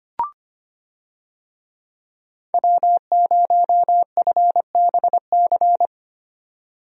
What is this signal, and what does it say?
Morse code, heard as a keyed tone, about 25 wpm, 700 Hz. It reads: W0FBC
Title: USA Call Signs - 5 Characters - 25wpm